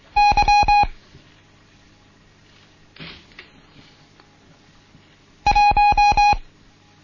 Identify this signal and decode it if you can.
Morse code, heard as a keyed tone, about 23 wpm, 830 Hz. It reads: Y 1
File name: Morsen / Kurs A 16